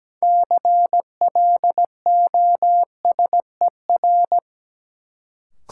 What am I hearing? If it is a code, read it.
Morse code, heard as a keyed tone, about 17 wpm, 700 Hz. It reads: CLOSER
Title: Sets of 5 Words Encore - New Word Drill - 17wpm